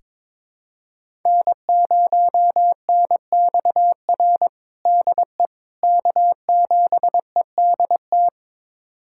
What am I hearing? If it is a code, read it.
Morse code, heard as a keyed tone, about 22 wpm, 700 Hz. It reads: N0NXR DE K7EDT
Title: USA Callsign DE Callsign - 5 Characters - 22wpm